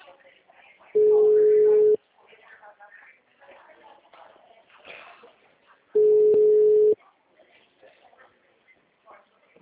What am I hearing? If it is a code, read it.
Ringback tone (Europe, 425 Hz), with speech.